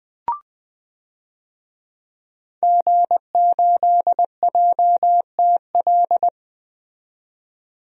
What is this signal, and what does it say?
Morse code, heard as a keyed tone, about 20 wpm, 700 Hz. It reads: G8JTL
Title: DXCC-100 Call Signs - 5 Characters - 20wpm